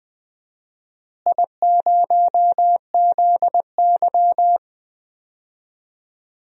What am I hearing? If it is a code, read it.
Morse code, heard as a keyed tone, about 20 wpm, 700 Hz. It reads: I0ZY
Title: DXCC-100 Call Signs - 4 Characters - Repeated 2x - 20wpm